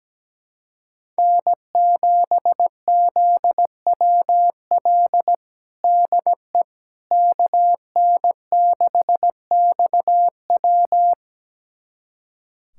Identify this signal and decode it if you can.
Morse code, heard as a keyed tone, about 17 wpm, 700 Hz. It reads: N7ZWL DE KN6XW
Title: USA Callsign DE Callsign - 5 Characters - 17wpm